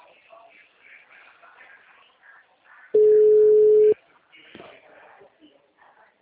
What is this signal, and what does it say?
Ringback tone (Europe, 425 Hz), with speech.